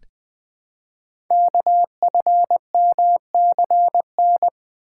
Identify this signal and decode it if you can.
Morse code, heard as a keyed tone, about 20 wpm, 700 Hz. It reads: KFMCN
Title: Code Groups - 5 Letters - 20wpm